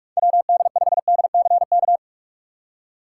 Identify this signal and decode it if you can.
Morse code, heard as a keyed tone, about 46 wpm, 700 Hz. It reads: WD5DCK
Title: USA Call Signs - 3x Character Spacing - 45wpm